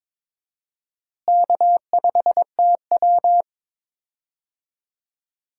Morse code, heard as a keyed tone, about 22 wpm, 700 Hz. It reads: K5TW
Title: USA Call Signs - 4 Characters - Repeated 3x - 22wpm